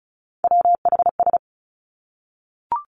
Morse code, heard as a keyed tone, about 36 wpm, 700 Hz. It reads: WHS
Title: Three Challenging Letter-Number Combinations 35wpm